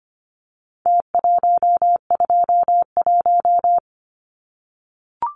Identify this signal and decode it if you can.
Morse code, heard as a keyed tone, about 25 wpm, 700 Hz. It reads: T121
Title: Numbers - Contest Style - 4-Digit 25wpm